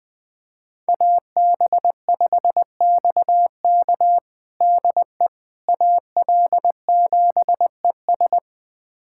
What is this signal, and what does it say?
Morse code, heard as a keyed tone, about 21 wpm, 700 Hz. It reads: AB5XK DE AL7ES
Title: USA Callsign DE Callsign - 5 Characters - 20wpm